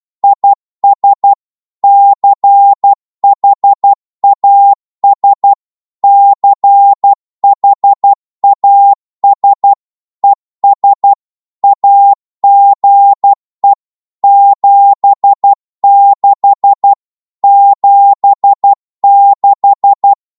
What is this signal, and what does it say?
Morse code, heard as a keyed tone, about 12 wpm, 800 Hz. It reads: IS CHAS CHAS ES AGE 76 76
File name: Morse / qso-12wpm